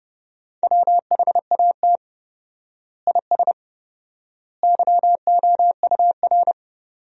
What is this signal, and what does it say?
Morse code, heard as a keyed tone, about 30 wpm, 700 Hz. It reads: WHAT IS YOUR
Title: 4x Word-Spacing - Sentences from Top 200 Words - 30wpm